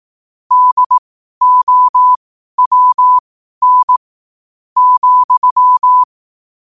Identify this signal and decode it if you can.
Morse code, heard as a keyed tone, about 18 wpm, spaced as for 11 wpm, 1000 Hz. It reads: DOWN,
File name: qotdfiles / 12wpm